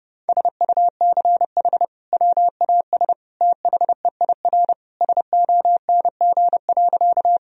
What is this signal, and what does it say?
Morse code, heard as a keyed tone, about 30 wpm, 700 Hz. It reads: SUCH WAS THEIR SONG.
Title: Sentences from Top 1000 Words - Review Part 2 - 30wpm